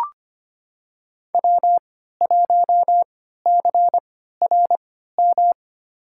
Morse code, heard as a keyed tone, about 25 wpm, spaced as for 14 wpm, 700 Hz. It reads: W1CRM
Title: USA Call Signs - 5 Characters - 3x Character Spacing - 25wpm